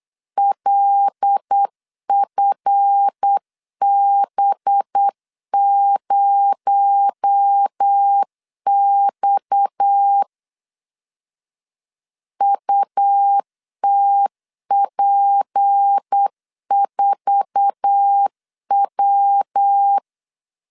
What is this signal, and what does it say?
Morse code, heard as a keyed tone, about 8 wpm, 800 Hz. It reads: LFB0X UTP4W
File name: LEZIONI / 8 YJ45 / 8 YJ4 8wpm 3246